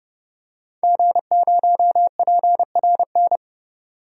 Morse code, heard as a keyed tone, about 32 wpm, 700 Hz. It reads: G0PRN
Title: DXCC-100 Call Signs - 5 Characters - 30wpm